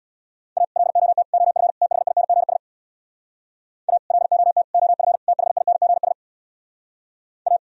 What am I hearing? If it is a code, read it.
Morse code, heard as a keyed tone, about 64 wpm, 700 Hz. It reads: A LOT OF THINGS A LOT OF THINGS A